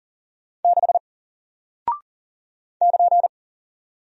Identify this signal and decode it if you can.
Morse code, heard as a keyed tone, about 40 wpm, 700 Hz. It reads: 6 <KN>
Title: Single Character (Letter:Number:Punctuation:Prosign) Alt 40wpm